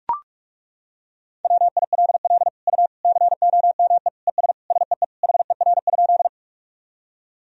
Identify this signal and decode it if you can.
Morse code, heard as a keyed tone, about 43 wpm, 700 Hz. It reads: WILL U COME ES SEE HER?
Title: Sentences from Top 100 Words and Abbreviations 45wpm